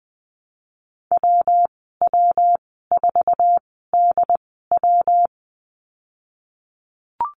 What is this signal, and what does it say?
Morse code, heard as a keyed tone, about 20 wpm, spaced as for 15 wpm, 700 Hz. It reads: WW4DW
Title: 122 Lesson 20g - Number 3 - Call Signs 20wpm